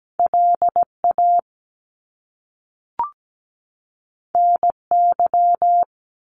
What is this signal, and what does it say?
Morse code, heard as a keyed tone, about 17 wpm, 700 Hz. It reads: LA NY
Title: US States 17wpm